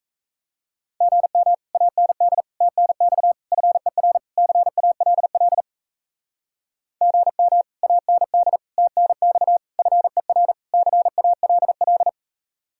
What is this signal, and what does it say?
Morse code, heard as a keyed tone, about 36 wpm, 700 Hz. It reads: GM AND TNX FER CALL GM AND TNX FER CALL